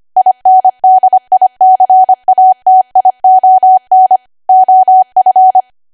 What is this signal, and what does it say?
Morse code, heard as a keyed tone, about 25 wpm, 750 Hz. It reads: INDICATION OF
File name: Archive / 25 WPM / 25WPM